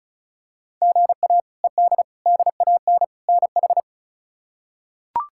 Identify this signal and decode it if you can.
Morse code, heard as a keyed tone, about 34 wpm, 700 Hz. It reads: GA ED DAN NH
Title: K1USN SST QSO Practice - 35wpm